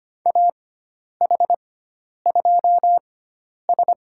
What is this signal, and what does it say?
Morse code, heard as a keyed tone, about 25 wpm, spaced as for 10 wpm, 700 Hz. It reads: AH2S
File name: Lesson 16g - Letter c - Call Signs 25wpm